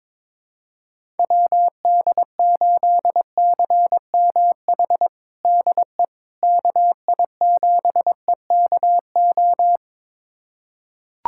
Morse code, heard as a keyed tone, about 22 wpm, 700 Hz. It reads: WD8CMH DE KI7EKO
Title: USA Callsign DE Callsign 22wpm